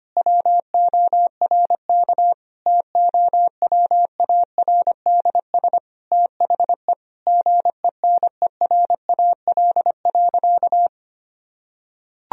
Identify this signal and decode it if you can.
Morse code, heard as a keyed tone, about 25 wpm, 700 Hz. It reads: WORK TOWARDS THE GENERAL.
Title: Sentences from Top 1000 Words - Review Part 3 - 25wpm